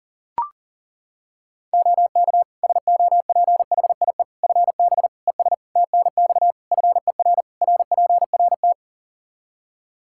Morse code, heard as a keyed tone, about 40 wpm, 700 Hz. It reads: OK SOPHIE FB ES TNX FER RPRT
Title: Practice QSO - Exchange 2 - Info - 40wpm